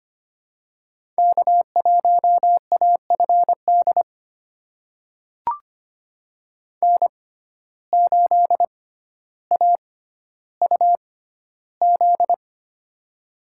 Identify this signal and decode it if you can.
Morse code, heard as a keyed tone, about 25 wpm, spaced as for 13 wpm, 690 Hz. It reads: K1AFD N8AUZ
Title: USA Call Signs - 5 Characters - 6x Character Spacing - 25wpm